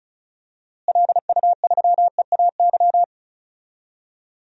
Morse code, heard as a keyed tone, about 36 wpm, 700 Hz. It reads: LU3EAY
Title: DXCC-100 Call Signs - 6 Characters - Repeated 3x - 35wpm